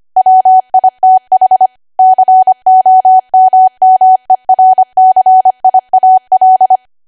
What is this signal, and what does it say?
Morse code, heard as a keyed tone, about 25 wpm, 750 Hz. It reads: WITH COMMERCIAL
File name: cwa / W1AW-Level3 / 25WPM